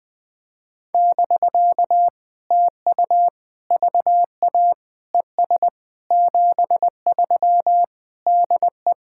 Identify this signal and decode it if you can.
Morse code, heard as a keyed tone, about 20 wpm, 700 Hz. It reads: <BK> TU VA ES 73 DE
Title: POTA QSO Practice - 20wpm